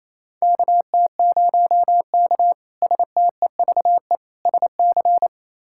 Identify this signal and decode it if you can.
Morse code, heard as a keyed tone, about 28 wpm, 700 Hz. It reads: KT0K STEVE SC
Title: K1USN SST QSO Practice - 28wpm